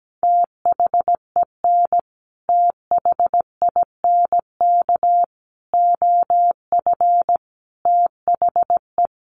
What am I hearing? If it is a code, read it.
Morse code, heard as a keyed tone, about 17 wpm, 700 Hz. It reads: THEN THINK OF THE
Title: Sentences from Top 1000 Words 17wpm